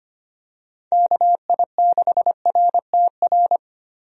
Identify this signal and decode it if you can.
Morse code, heard as a keyed tone, about 25 wpm, 700 Hz. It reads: KI6RTR
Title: USA Call Signs - 6 Characters - Repeated 2x - 25wpm